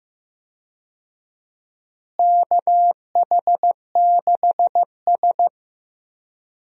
Morse code, heard as a keyed tone, about 15 wpm, 700 Hz. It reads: KH6S
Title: DXCC-100 Call Signs - 4 Characters - Repeated 3x - 15wpm